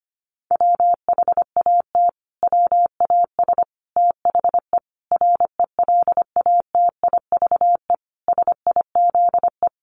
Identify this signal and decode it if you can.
Morse code, heard as a keyed tone, about 25 wpm, 700 Hz. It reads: WHAT WAS THE RELATIVE SIZE
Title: Sentences from Top 1400 Words 25wpm